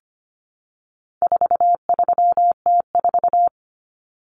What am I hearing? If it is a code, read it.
Morse code, heard as a keyed tone, about 25 wpm, 700 Hz. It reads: 43T4
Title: Common Cut Numbers - 4-Digit 25wpm